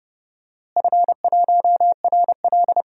Morse code, heard as a keyed tone, about 30 wpm, 700 Hz. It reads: F1RL
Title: DXCC-100 Call Signs - 4 Characters - Repeated 2x - 30wpm